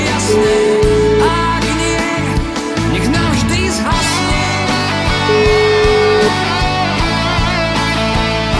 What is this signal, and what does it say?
Ringback tone (Europe, 425 Hz), with noise.